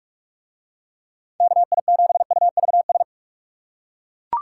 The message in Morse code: KI7UVS

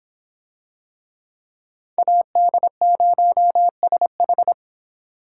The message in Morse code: AD0SH

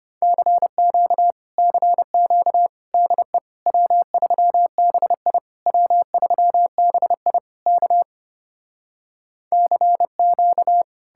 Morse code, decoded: CQ CQ DE W3BI W3BI K CQ

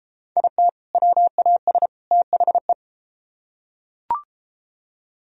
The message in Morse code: IT WAS THE

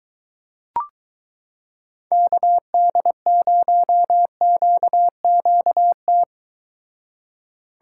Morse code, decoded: KD0QQT

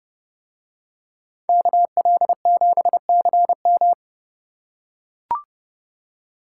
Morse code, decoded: KL7CM

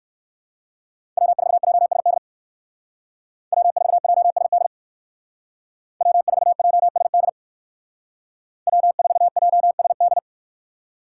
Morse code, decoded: W4JSD W4JSD W4JSD W4JSD

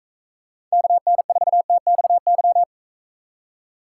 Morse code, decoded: KN4TXY

